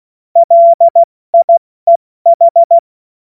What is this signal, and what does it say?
Morse code, heard as a keyed tone, about 16 wpm, spaced as for 14 wpm, 680 Hz. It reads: LIEH
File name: Morsekurs / Witze / Witz 07. März 2023 70 F80